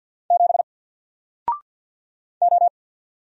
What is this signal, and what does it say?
Morse code, heard as a keyed tone, about 50 wpm, 700 Hz. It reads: 7 O
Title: Single Letter-Number 50wpm